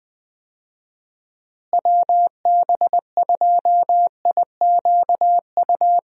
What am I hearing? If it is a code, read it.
Morse code, heard as a keyed tone, about 20 wpm, 700 Hz. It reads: WB2IQU
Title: USA Call Signs - 6 Characters - Repeated 3x - 20wpm